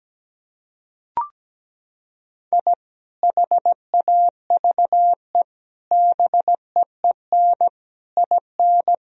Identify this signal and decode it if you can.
Morse code, heard as a keyed tone, about 17 wpm, 700 Hz. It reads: I HAVE BEEN IN
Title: Sentences from Top 1000 Words - Review Part 3 - 17wpm